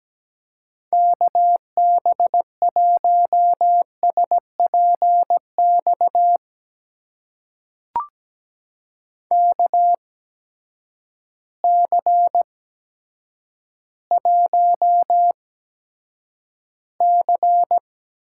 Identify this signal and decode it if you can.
Morse code, heard as a keyed tone, about 17 wpm, spaced as for 9 wpm, 700 Hz. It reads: KB1SPX KC1C
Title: USA Call Signs - 6 Characters - 8x Character Spacing - 17wpm